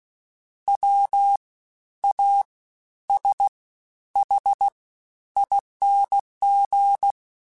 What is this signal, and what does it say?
Morse code, heard as a keyed tone, about 16 wpm, spaced as for 10 wpm, 800 Hz. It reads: WASHING
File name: Special INGs-mp3s / Special INGs-mp3s / SlowToFaster ing @15